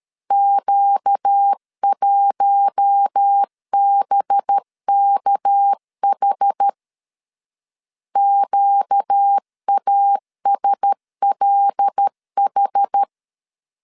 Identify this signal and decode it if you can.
Morse code, heard as a keyed tone, about 13 wpm, 800 Hz. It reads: Q1BKH QASLH
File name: LEZIONI / 6 PKX0 / 6 PK 12wpm 2006